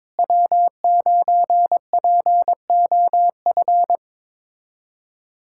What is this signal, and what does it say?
Morse code, heard as a keyed tone, about 22 wpm, 700 Hz. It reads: W9POF
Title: USA Callsign DE Callsign - 5 Characters - 22wpm